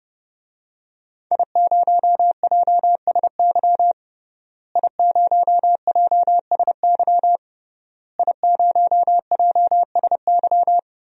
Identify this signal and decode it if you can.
Morse code, heard as a keyed tone, about 30 wpm, 700 Hz. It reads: I0JSY I0JSY I0JSY
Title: DXCC-100 Call Signs - 5 Characters - Repeated 3x - 30wpm